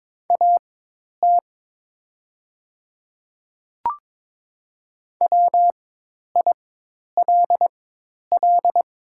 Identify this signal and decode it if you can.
Morse code, heard as a keyed tone, about 23 wpm, spaced as for 10 wpm, 700 Hz. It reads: AT WILL
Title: 4x Character Spacing - Top 100 Words - 22wpm